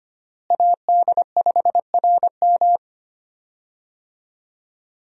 Morse code, heard as a keyed tone, about 25 wpm, 700 Hz. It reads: AD5RM
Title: USA Call Signs - 5 Characters - Repeated 2x - 25wpm